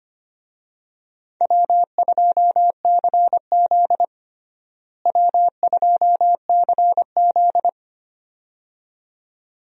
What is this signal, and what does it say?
Morse code, heard as a keyed tone, about 25 wpm, 700 Hz. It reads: W2CZ W2CZ
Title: USA Call Signs - 4 Characters - Repeated 2x - 25wpm